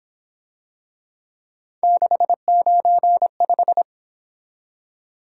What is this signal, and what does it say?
Morse code, heard as a keyed tone, about 26 wpm, 700 Hz. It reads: 695
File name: Speed-Racing - Three Letter-Number Combinations - 22wpm